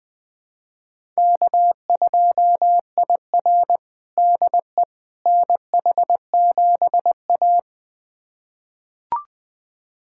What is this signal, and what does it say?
Morse code, heard as a keyed tone, about 20 wpm, 700 Hz. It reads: K2IR DE NH7A
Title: USA Callsign DE Callsign - 4 Characters - 20wpm